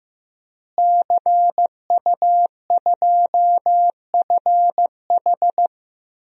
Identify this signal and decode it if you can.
Morse code, heard as a keyed tone, about 16 wpm, 700 Hz. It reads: CU2FH
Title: DXCC-100 Call Signs - 5 Characters - Repeated 3x - 15wpm